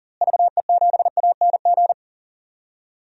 Morse code, heard as a keyed tone, about 39 wpm, 700 Hz. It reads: VE7ANG
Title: DXCC-100 Call Signs - 6 Characters - Repeated 3x - 40wpm